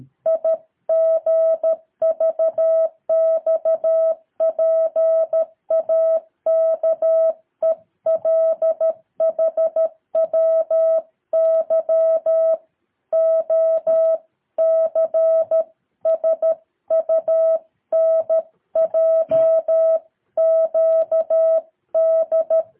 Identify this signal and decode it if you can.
Morse code, heard as a keyed tone, about 13 wpm, 650 Hz. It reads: IGVXPAKELHWY O C SUNJ QD
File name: jh8cbh / 2023cw